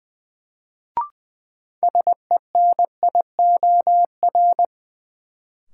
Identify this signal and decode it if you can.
Morse code, heard as a keyed tone, about 20 wpm, 700 Hz. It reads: SENIOR
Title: Sets of 5 Words Encore - New Word Drill - 20wpm